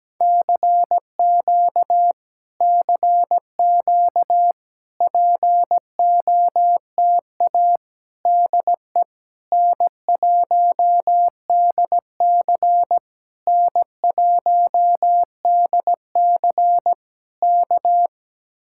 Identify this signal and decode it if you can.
Morse code, heard as a keyed tone, about 17 wpm, 700 Hz. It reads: CQ CQ POTA DE N1DC N1DC K
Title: Speed-Racing Common QSO Phrases - 17wpm